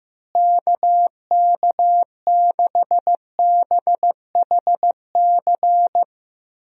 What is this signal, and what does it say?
Morse code, heard as a keyed tone, about 15 wpm, 700 Hz. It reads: KK6BHC
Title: USA Call Signs - 6 Characters - Repeated 2x - 15wpm